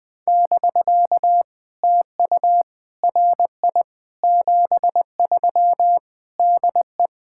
Morse code, heard as a keyed tone, about 19 wpm, 700 Hz. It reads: <BK> TU RI 73 DE
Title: POTA QSO Practice - 20wpm